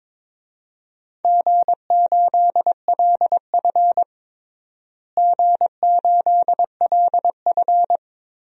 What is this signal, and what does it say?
Morse code, heard as a keyed tone, about 22 wpm, 700 Hz. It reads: G8LF G8LF
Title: DXCC-100 Call Signs - 4 Characters - Repeated 3x - 22wpm